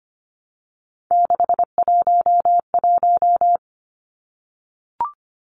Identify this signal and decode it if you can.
Morse code, heard as a keyed tone, about 25 wpm, 700 Hz. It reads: 611
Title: Numbers - 3-Digit 25wpm